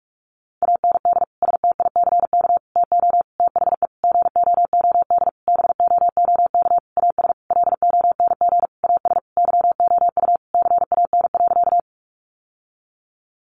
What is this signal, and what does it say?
Morse code, heard as a keyed tone, about 44 wpm, 700 Hz. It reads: AND STICK TO THE GOOD BOOK AS LONG AS YOU CAN.
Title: Sentences from Top 1200 Words 45wpm